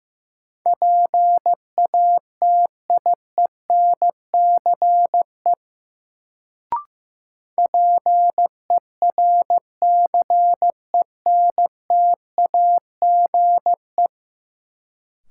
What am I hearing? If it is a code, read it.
Morse code, heard as a keyed tone, about 15 wpm, 700 Hz. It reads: PATIENCE PERCENTAGE
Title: Sentences from Top 1300 Words - New Word Drill Alt - 15wpm